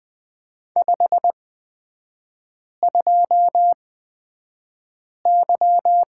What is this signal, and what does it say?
Morse code, heard as a keyed tone, about 20 wpm, spaced as for 5 wpm, 700 Hz. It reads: 52Y
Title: 250 Lesson 40e - BK - Three Characters 20wpm